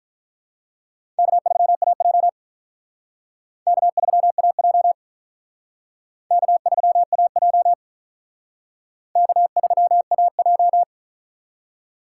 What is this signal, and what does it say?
Morse code, heard as a keyed tone, about 44 wpm, 700 Hz. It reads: K3AJ K3AJ K3AJ K3AJ